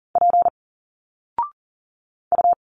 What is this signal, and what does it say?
Morse code, heard as a keyed tone, about 39 wpm, 700 Hz. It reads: P U
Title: Single Letters 40wpm